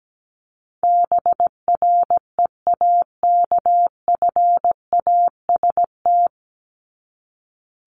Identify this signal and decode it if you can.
Morse code, heard as a keyed tone, about 17 wpm, 700 Hz. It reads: BREAKFAST
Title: Top 1251-1500 Words 17wpm